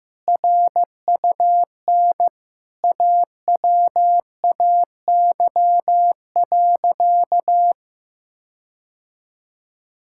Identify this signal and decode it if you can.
Morse code, heard as a keyed tone, about 15 wpm, 700 Hz. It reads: RUN AWAY.
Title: Sentences from Top 1600 Words 15wpm